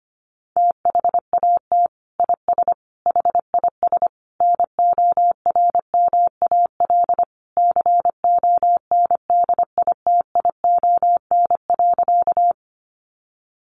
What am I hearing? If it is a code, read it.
Morse code, heard as a keyed tone, about 25 wpm, 700 Hz. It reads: THAT IS HIS NORMAL CONDITION.
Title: Sentences from Top 1200 Words 25wpm